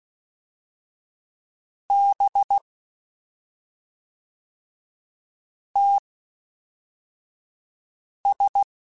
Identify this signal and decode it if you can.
Morse code, heard as a keyed tone, about 16 wpm, spaced as for 3 wpm, 800 Hz. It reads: BTS
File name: CW Comparisons / CW Comparisons-mp3s / CW Comparisons Broken Characters @5